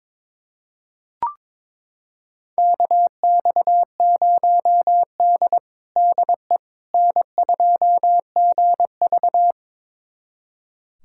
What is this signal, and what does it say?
Morse code, heard as a keyed tone, about 22 wpm, 700 Hz. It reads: KX0D DE N2GV
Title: USA Callsign DE Callsign - 4 Characters - 22wpm